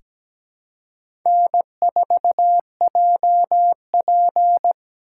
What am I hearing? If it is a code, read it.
Morse code, heard as a keyed tone, about 17 wpm, 700 Hz. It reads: N4JP
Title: USA Call Signs - 4 Characters - 6x Character Spacing - 17wpm